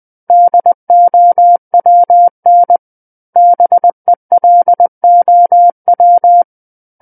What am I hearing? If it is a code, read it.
Morse code, heard as a keyed tone, about 20 wpm, 700 Hz. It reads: DOWN BELOW
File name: 20 WPM